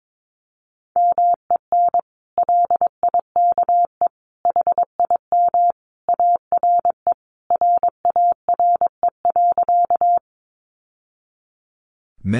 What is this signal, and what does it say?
Morse code, heard as a keyed tone, about 22 wpm, 700 Hz. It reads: MEN LIKE HIM ARE RARE.
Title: Sentences from Top 1100 Words 22wpm